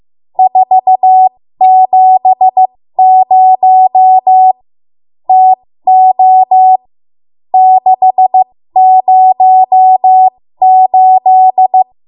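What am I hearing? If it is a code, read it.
Morse code, heard as a keyed tone, about 15 wpm, spaced as for 13 wpm, 750 Hz. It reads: 470 TO 608
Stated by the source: ARRL Morse code practice at 13 WPM.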